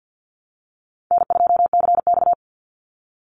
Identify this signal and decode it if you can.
Morse code, heard as a keyed tone, about 50 wpm, 700 Hz. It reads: N2CX
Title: Call Signs Repeated 2x - 50wpm